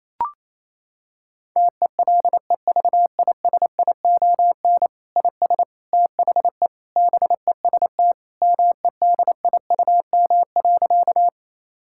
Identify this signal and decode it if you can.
Morse code, heard as a keyed tone, about 28 wpm, 700 Hz. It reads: TELEVISION IS THE BEST MEDIUM.
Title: Sentences from Top 1000 Words - Review Part 6 - 28wpm